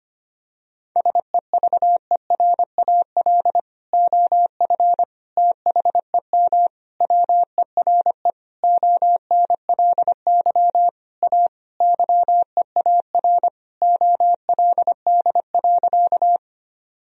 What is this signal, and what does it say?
Morse code, heard as a keyed tone, about 25 wpm, 700 Hz. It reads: SEVERAL OF THEM WERE ONLY A YEAR OLD.
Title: Sentences from Top 1000 Words - Review Part 2 - 25wpm